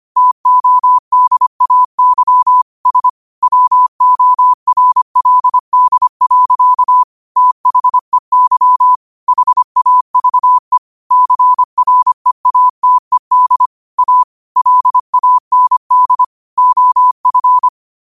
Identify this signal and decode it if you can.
Morse code, heard as a keyed tone, about 25 wpm, 1000 Hz. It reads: TODAY S WORLD. THEY HAVE CREATED A LAND OF
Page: Quotes for Fri, 15 Aug 2025 in Morse Code at 25 words per minute.